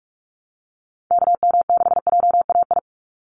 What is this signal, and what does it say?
Morse code, heard as a keyed tone, about 44 wpm, 700 Hz. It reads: KM6JAI
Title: USA Call Signs 45wpm